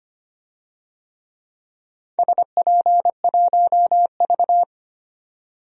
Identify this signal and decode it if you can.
Morse code, heard as a keyed tone, about 25 wpm, 700 Hz. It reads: SP1V
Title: DXCC-100 Call Signs - 4 Characters - 25wpm